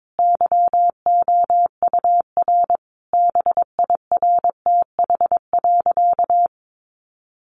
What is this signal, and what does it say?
Morse code, heard as a keyed tone, about 22 wpm, 700 Hz. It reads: YOUR BIRTH.
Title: Sentences from Top 1300 Words 22wpm